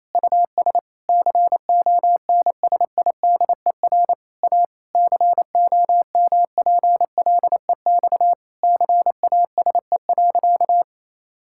Morse code, heard as a keyed tone, about 28 wpm, 700 Hz. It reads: US CONSIDER A COMPLEX CASE.
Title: Sentences from Top 1000 Words - Review Part 6 - 28wpm